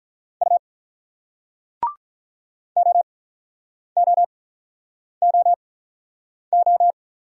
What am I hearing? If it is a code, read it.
Morse code, heard as a keyed tone, about 48 wpm, spaced as for 9 wpm, 700 Hz. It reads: U OOOO